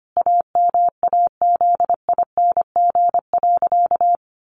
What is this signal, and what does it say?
Morse code, heard as a keyed tone, about 25 wpm, 700 Hz. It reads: AMAZING.
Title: Sentences from Top 1500 Words 25wpm